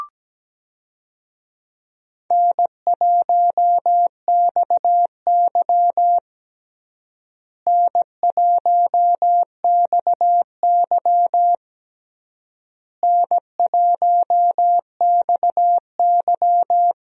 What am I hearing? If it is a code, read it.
Morse code, heard as a keyed tone, about 17 wpm, 700 Hz. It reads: N1XY N1XY N1XY
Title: USA Call Signs - 4 Characters - Repeated 3x - 17wpm